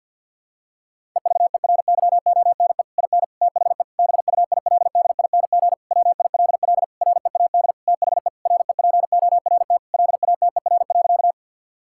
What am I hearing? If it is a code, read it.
Morse code, heard as a keyed tone, about 52 wpm, 700 Hz. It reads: EVERYONE IN THE BUILDING WILL READ THE REPORT LATER.
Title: Sentences from Top 1000 Words - Review Part 6 - 50wpm